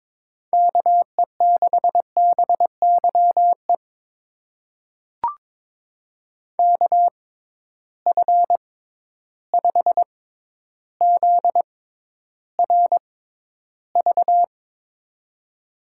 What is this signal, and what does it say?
Morse code, heard as a keyed tone, about 22 wpm, spaced as for 11 wpm, 700 Hz. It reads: KE6BYE KF5ZRV
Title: USA Call Signs - 6 Characters - 6x Character Spacing - 22wpm